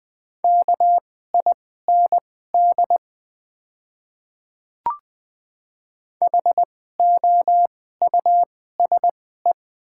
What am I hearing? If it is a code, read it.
Morse code, heard as a keyed tone, about 20 wpm, spaced as for 14 wpm, 700 Hz. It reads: KIND HOUSE
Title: 2x Character Spacing - Top 500 Words - 20wpm